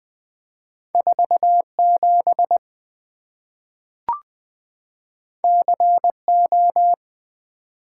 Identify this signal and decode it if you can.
Morse code, heard as a keyed tone, about 20 wpm, 700 Hz. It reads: 4 7 C O
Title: Two Letter-Number Combinations - 20wpm